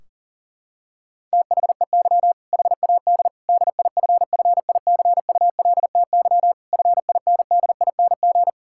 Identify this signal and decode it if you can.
Morse code, heard as a keyed tone, about 37 wpm, 700 Hz. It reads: THEY HAD DIFFICULTY FINDING
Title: Sentences from Top 1600 Words 40wpm